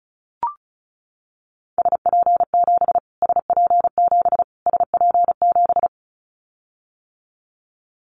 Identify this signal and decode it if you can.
Morse code, heard as a keyed tone, about 34 wpm, 700 Hz. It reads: SP7 SP7 SP7
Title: Callsign Prefixes for Top 25 DXCC - Repeated 3x - 35wpm